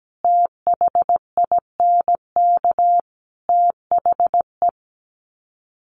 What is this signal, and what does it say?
Morse code, heard as a keyed tone, about 17 wpm, 700 Hz. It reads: THINK THE
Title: Sets of 5 Words Encore 17wpm